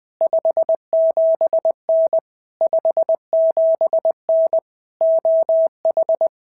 Morse code, heard as a keyed tone, about 20 wpm, 650 Hz. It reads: 57N 57N OH
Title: POTA QSO Practice - 20wpm